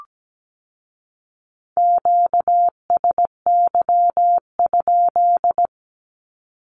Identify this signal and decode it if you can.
Morse code, heard as a keyed tone, about 17 wpm, 700 Hz. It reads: QSY?
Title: Q-Codes with Spoken Meaning - 17wpm